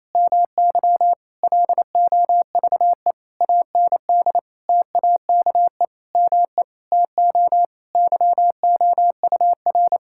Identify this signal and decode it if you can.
Morse code, heard as a keyed tone, about 28 wpm, 700 Hz. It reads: MY LOVE AND TAKE ME TO YOUR
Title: Sentences from Top 1000 Words - Review Part 6 - 28wpm